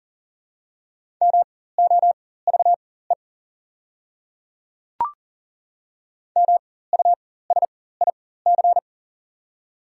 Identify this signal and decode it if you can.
Morse code, heard as a keyed tone, about 40 wpm, spaced as for 19 wpm, 690 Hz. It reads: MOVE MUSIC
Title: 4x Character Spacing - Top 300 Words - 40wpm